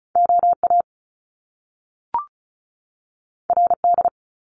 Morse code, heard as a keyed tone, about 36 wpm, 700 Hz. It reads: OA RD
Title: Top Two Letter Combinations in Words 35wpm